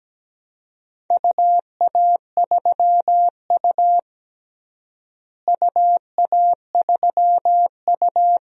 Morse code, heard as a keyed tone, about 17 wpm, 700 Hz. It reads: UA3U UA3U
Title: DXCC-100 Call Signs - 4 Characters - Repeated 2x - 17wpm